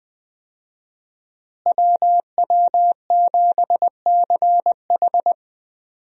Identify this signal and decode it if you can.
Morse code, heard as a keyed tone, about 20 wpm, 700 Hz. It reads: WW7CH
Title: USA Call Signs - 5 Characters - Repeated 2x - 20wpm